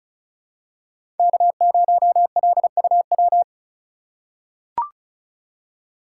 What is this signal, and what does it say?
Morse code, heard as a keyed tone, about 34 wpm, 700 Hz. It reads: K0LUW
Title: K1USN SST QSO Practice - 35wpm